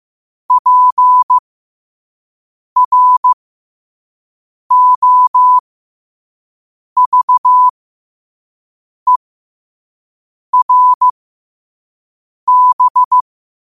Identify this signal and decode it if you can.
Morse code, heard as a keyed tone, about 15 wpm, spaced as for 5 wpm, 1000 Hz. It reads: PROVERB
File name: qotdfiles / 5wpm